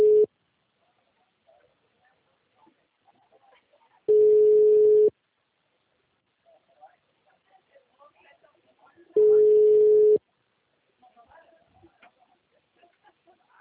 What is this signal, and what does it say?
Ringback tone (Europe, 425 Hz), with speech.